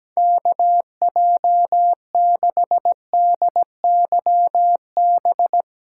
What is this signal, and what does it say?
Morse code, heard as a keyed tone, about 17 wpm, 700 Hz. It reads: KJ6DYB
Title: USA Callsign DE Callsign 17wpm